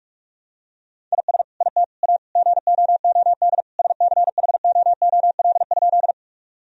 Morse code, heard as a keyed tone, about 46 wpm, 700 Hz. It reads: IS IT A GOOD SCHOOL?
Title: Sentences from Top 1000 Words - Review Part 1 - 45wpm